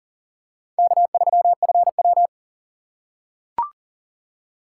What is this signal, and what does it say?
Morse code, heard as a keyed tone, about 40 wpm, 700 Hz. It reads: K3FW
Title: USA Call Signs - 4 Characters - 2x Character Spacing - 40wpm